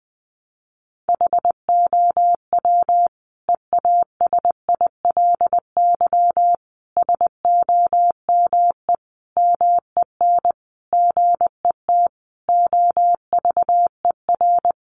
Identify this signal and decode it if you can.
Morse code, heard as a keyed tone, about 20 wpm, 700 Hz. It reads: HOW EASILY SOME MEN GET OVER
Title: Sentences from Top 900 Words 20wpm